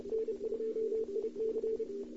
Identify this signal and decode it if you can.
Morse code, heard as a keyed tone, about 29 wpm, 440 Hz. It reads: N2IC
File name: CW / 2008-ss-cw-k5tr-op-n5rz -02:00